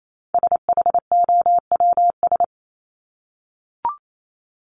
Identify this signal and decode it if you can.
Morse code, heard as a keyed tone, about 28 wpm, 700 Hz. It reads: SHOWS
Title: ES:S Suffix - Top 100 Words with Base Words 28wpm